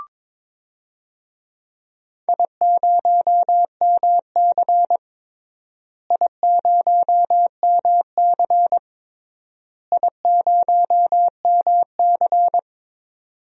Morse code, heard as a keyed tone, about 22 wpm, 700 Hz. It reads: I0MC I0MC I0MC
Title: DXCC-100 Call Signs - 4 Characters - Repeated 3x - 22wpm